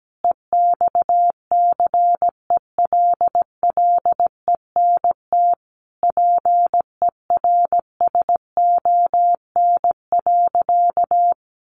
Morse code, heard as a keyed tone, about 17 wpm, 700 Hz. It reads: EXCELLENT PERSON.
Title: Sentences from Top 1000 Words 17wpm